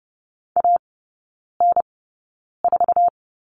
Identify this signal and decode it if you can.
Morse code, heard as a keyed tone, about 30 wpm, spaced as for 9 wpm, 700 Hz. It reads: AN4
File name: Lesson 19e - Letter w - Three Characters 30wpm